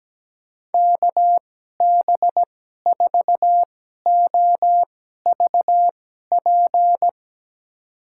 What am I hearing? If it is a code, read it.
Morse code, heard as a keyed tone, about 17 wpm, spaced as for 12 wpm, 700 Hz. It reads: KB4OVP
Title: USA Call Signs - 6 Characters - 2x Character Spacing - 17wpm